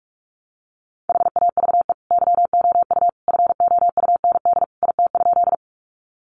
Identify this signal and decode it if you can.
Morse code, heard as a keyed tone, about 44 wpm, 700 Hz. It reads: HAVE YOU FOUND IT?
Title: Sentences from Top 300 Words 45wpm